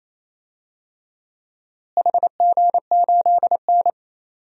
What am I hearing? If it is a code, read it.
Morse code, heard as a keyed tone, about 28 wpm, 700 Hz. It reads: HG8N
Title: DXCC-100 Call Signs - 4 Characters - Repeated 2x - 28wpm